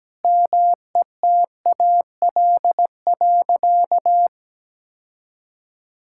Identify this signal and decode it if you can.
Morse code, heard as a keyed tone, about 17 wpm, 700 Hz. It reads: METAL.
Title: Sentences from Top 1000 Words - Review Part 6 - 17wpm